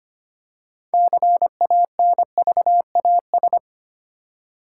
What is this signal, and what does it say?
Morse code, heard as a keyed tone, about 25 wpm, 700 Hz. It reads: CANVAS
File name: practice / 25wpm / Easy Words with Hints - 25wpm